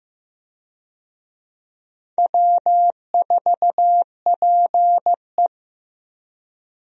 Morse code, heard as a keyed tone, about 15 wpm, 700 Hz. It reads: W4PE
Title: USA Call Signs - 4 Characters - Repeated 3x - 15wpm